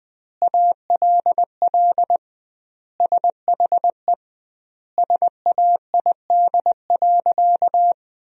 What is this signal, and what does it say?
Morse code, heard as a keyed tone, about 20 wpm, 700 Hz. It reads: ALL SHE SAID.
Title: 2x Word-Spacing - Sentences from Top 100 Words - 20wpm